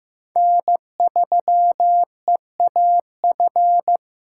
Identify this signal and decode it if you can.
Morse code, heard as a keyed tone, about 15 wpm, 700 Hz. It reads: N3EAF
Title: USA Call Signs - 5 Characters - 6x Character Spacing - 15wpm